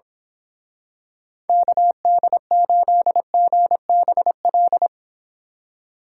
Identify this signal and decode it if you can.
Morse code, heard as a keyed tone, about 26 wpm, 700 Hz. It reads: KD8GBL
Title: Speed Racing - USA 2x3 Call Signs 22wpm